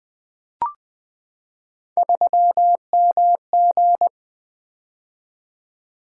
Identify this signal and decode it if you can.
Morse code, heard as a keyed tone, about 21 wpm, 700 Hz. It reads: 3MG
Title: Three Letter-Number Combinations - 20wpm